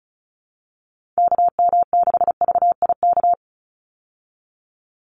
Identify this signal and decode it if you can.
Morse code, heard as a keyed tone, about 34 wpm, 700 Hz. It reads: KM6VIK
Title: USA Call Signs 35wpm